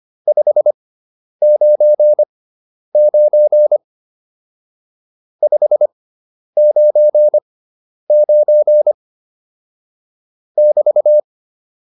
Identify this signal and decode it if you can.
Morse code, heard as a keyed tone, about 25 wpm, spaced as for 10 wpm, 600 Hz. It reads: 599 599 <BT>
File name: QSO208 10